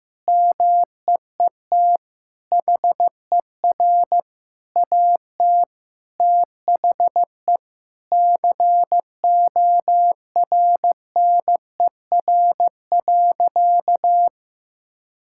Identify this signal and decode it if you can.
Morse code, heard as a keyed tone, about 16 wpm, 700 Hz. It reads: MEET HER AT THE CORNER.
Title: Sentences from Top 1600 Words 15wpm